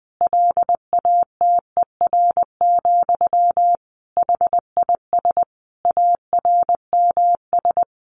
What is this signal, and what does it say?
Morse code, heard as a keyed tone, about 20 wpm, 700 Hz. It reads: LATER, HIS ARMS
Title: A Sound Of Thunder 20wpm